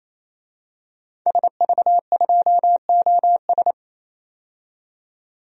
Morse code, heard as a keyed tone, about 28 wpm, 700 Hz. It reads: SV2OS
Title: DXCC-100 Call Signs - 5 Characters - 28wpm